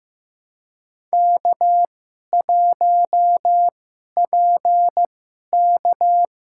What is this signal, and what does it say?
Morse code, heard as a keyed tone, about 15 wpm, spaced as for 11 wpm, 700 Hz. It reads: K1PK
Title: USA Call Signs - 4 Characters - 2x Character Spacing - 15wpm